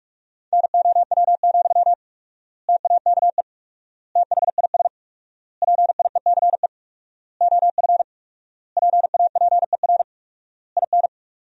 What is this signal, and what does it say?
Morse code, heard as a keyed tone, about 43 wpm, 700 Hz. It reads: NOW, TAKE THIS PIECE OF PAPER IN
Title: 4x Word-Spacing - Sentences from Top 400 Words - 45wpm